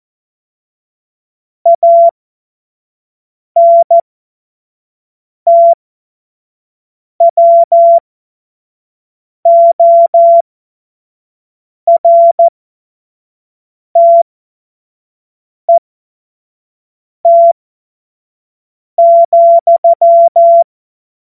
Morse code, heard as a keyed tone, about 14 wpm, spaced as for 5 wpm, 680 Hz. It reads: ANTWORTET,
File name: Morsekurs / Witze / Witz 25. Oktober 2022 25 F70